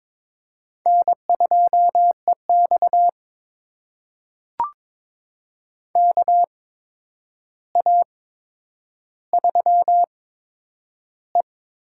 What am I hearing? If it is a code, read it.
Morse code, heard as a keyed tone, about 22 wpm, spaced as for 9 wpm, 700 Hz. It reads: N2EX KA3E
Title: USA Call Signs - 4 Characters - 8x Character Spacing - 22wpm